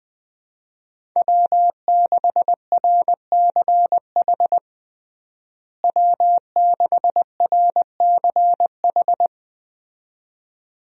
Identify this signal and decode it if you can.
Morse code, heard as a keyed tone, about 20 wpm, 700 Hz. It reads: W6RCH W6RCH
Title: USA Call Signs - 5 Characters - Repeated 2x - 20wpm